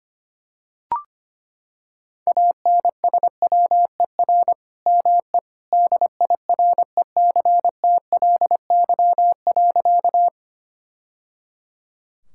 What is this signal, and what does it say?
Morse code, heard as a keyed tone, about 25 wpm, 700 Hz. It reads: ANSWER ME DIRECTLY.
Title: Sentences from Top 1000 Words - Review Part 5 - 25wpm